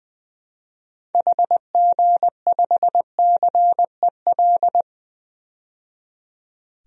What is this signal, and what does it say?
Morse code, heard as a keyed tone, about 21 wpm, 700 Hz. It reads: HG5CEL
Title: DXCC-100 Call Signs - 6 Characters - 20wpm